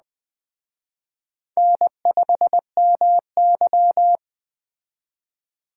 Morse code, heard as a keyed tone, about 19 wpm, 700 Hz. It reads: N5MY